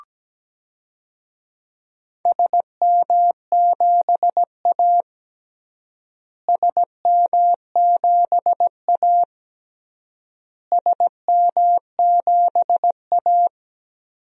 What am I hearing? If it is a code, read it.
Morse code, heard as a keyed tone, about 17 wpm, 700 Hz. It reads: SM7A SM7A SM7A
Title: DXCC-100 Call Signs - 4 Characters - Repeated 3x - 17wpm